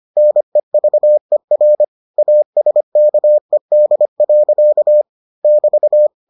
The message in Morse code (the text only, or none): NEVER ASKED. <BT>